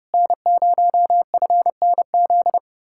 Morse code, heard as a keyed tone, about 33 wpm, 700 Hz. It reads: N0FNZ